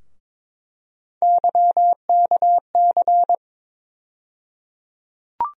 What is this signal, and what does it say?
Morse code, heard as a keyed tone, about 21 wpm, 700 Hz. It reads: YKC
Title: USA Call Sign Postfixes - 3 Letters 22wpm